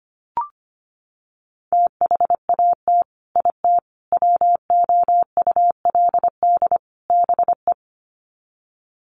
Morse code, heard as a keyed tone, about 25 wpm, 700 Hz. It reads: THAT IT WOULD BE
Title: Sets of 4 Words 25wpm